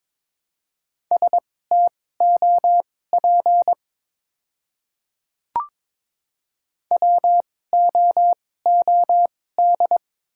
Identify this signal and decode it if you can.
Morse code, heard as a keyed tone, about 21 wpm, spaced as for 16 wpm, 700 Hz. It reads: STOP WOOD
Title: 2x Character Spacing - Top 300 Words - 22wpm